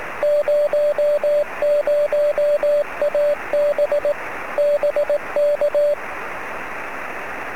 Morse code, heard as a keyed tone, about 18 wpm, 590 Hz, with radio noise. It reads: 00AB BK
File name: sm7lcb / 07-oh0jfp 23cm